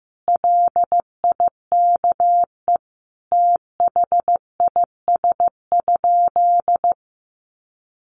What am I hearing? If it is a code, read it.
Morse code, heard as a keyed tone, about 15 wpm, 700 Hz. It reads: LIKE THIS?
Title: Sentences from Top 600 Words 15wpm